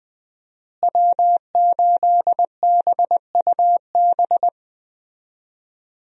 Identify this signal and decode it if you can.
Morse code, heard as a keyed tone, about 20 wpm, 700 Hz. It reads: W8BUB
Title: USA Call Signs - 5 Characters - Repeated 2x - 20wpm